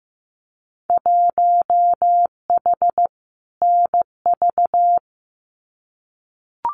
Morse code, heard as a keyed tone, about 15 wpm, 700 Hz. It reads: 1H NV
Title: Winter Field Day Exchanges 15wpm